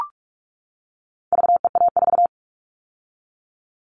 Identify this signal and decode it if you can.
Morse code, heard as a keyed tone, about 44 wpm, 700 Hz. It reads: VEA4
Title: All Cut Numbers - 4-Digit 45wpm